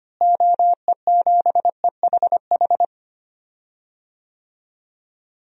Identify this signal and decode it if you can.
Morse code, heard as a keyed tone, about 25 wpm, 700 Hz. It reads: OE7EHH
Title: DXCC-100 Call Signs - 6 Characters - Repeated 3x - 25wpm